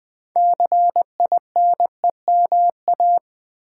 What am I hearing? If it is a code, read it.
Morse code, heard as a keyed tone, about 20 wpm, 700 Hz. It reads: CINEMA
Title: 6 Letter Words - First 3 Letters Given - 20wpm